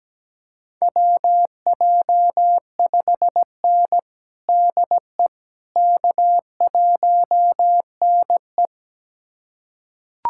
Morse code, heard as a keyed tone, about 17 wpm, 700 Hz. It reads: WJ5N DE K1NE
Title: USA Callsign DE Callsign - 4 Characters - 17wpm